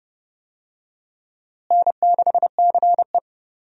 Morse code, heard as a keyed tone, about 30 wpm, 700 Hz. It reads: N6CE
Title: USA Call Signs - 4 Characters - Repeated 3x - 30wpm